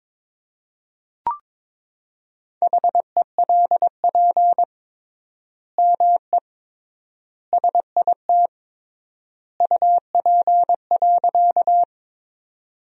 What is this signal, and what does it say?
Morse code, heard as a keyed tone, about 22 wpm, 700 Hz. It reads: HELP ME SIT UP.
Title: 3x Word-Spacing - Sentences from Top 500 Words - 22wpm